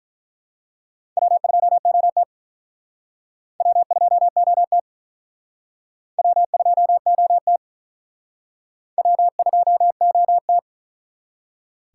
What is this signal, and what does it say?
Morse code, heard as a keyed tone, about 42 wpm, 700 Hz. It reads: W2OT W2OT W2OT W2OT
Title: Speed Racing - USA 1x2 Call Signs 35wpm